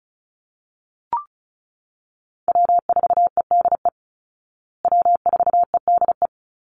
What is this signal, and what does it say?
Morse code, heard as a keyed tone, about 34 wpm, 700 Hz. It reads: W4EDE W4EDE
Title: Call Signs Repeated 2x - 35wpm